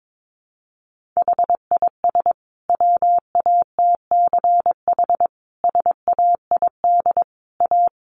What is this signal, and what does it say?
Morse code, heard as a keyed tone, about 22 wpm, 700 Hz. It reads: HIS WATCH SAID A
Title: Sentences from Top 1300 Words 22wpm